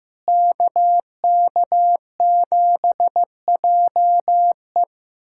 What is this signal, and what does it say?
Morse code, heard as a keyed tone, about 15 wpm, 700 Hz. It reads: KK7JE
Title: USA Callsign DE Callsign - 5 Characters - 15wpm